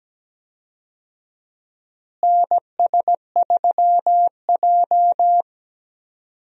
Morse code, heard as a keyed tone, about 17 wpm, 700 Hz. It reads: NS3J
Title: USA Call Signs - 4 Characters - Repeated 3x - 17wpm